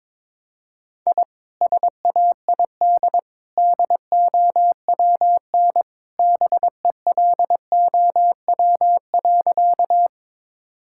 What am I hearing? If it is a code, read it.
Morse code, heard as a keyed tone, about 22 wpm, 700 Hz. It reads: I SAID DOWN BELOW.
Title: Sentences from Top 1000 Words - Review Part 5 - 22wpm